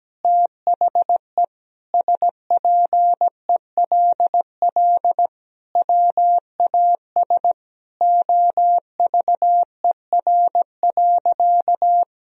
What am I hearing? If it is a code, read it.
Morse code, heard as a keyed tone, about 17 wpm, 700 Hz. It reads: THE SPELL WAS OVER.
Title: Sentences from Top 1000 Words - Review Part 1 - 17wpm